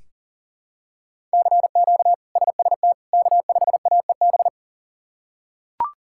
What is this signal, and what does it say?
Morse code, heard as a keyed tone, about 40 wpm, 700 Hz. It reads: CQ SST K5AEB
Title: K1USN SST QSO Practice - 40wpm